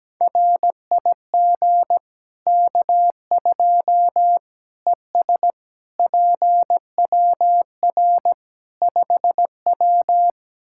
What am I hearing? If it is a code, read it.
Morse code, heard as a keyed tone, about 17 wpm, 700 Hz. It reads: RIG K2 ES PWR 5W
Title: Practice QSO - Exchange 2 - Info - 17wpm